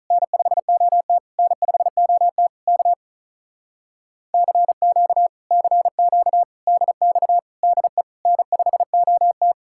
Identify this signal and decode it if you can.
Morse code, heard as a keyed tone, about 48 wpm, 700 Hz. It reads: N5OT N5OT K CQ CQ DX DE N5OT